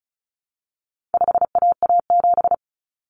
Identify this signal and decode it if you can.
Morse code, heard as a keyed tone, about 34 wpm, 700 Hz. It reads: 5AA7
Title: Common Cut Numbers - 4-Digit 35wpm